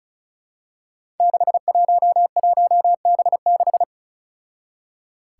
Morse code, heard as a keyed tone, about 34 wpm, 700 Hz. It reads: 611B6
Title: Five Challenging Letter-Number Combinations 35wpm